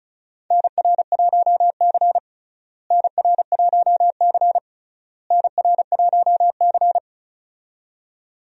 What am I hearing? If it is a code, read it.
Morse code, heard as a keyed tone, about 36 wpm, 700 Hz. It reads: NR1C NR1C NR1C
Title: USA Call Signs - 4 Characters - Repeated 3x - 35wpm